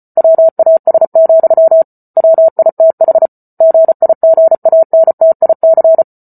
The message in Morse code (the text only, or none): WAS, WITH GIGANTIC